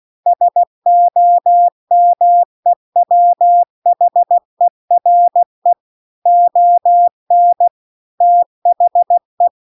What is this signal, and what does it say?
Morse code, heard as a keyed tone, about 16 wpm, 700 Hz. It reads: SOMEWHERE ON THE